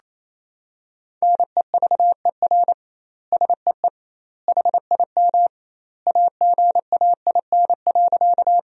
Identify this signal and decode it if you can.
Morse code, heard as a keyed tone, about 28 wpm, 700 Hz. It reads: NEVER SEE HIM AGAIN.
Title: 2x Word-Spacing - Sentences from Top 300 Words - 28wpm